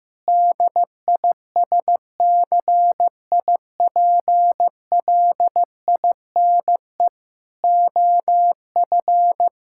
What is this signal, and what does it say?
Morse code, heard as a keyed tone, about 16 wpm, 700 Hz. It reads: DISCIPLINE OF
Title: Sentences from Top 1000 Words - Review Part 6 - 15wpm